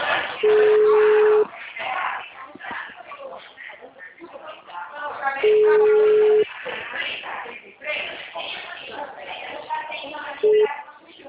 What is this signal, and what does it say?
Ringback tone (Europe, 425 Hz), with speech.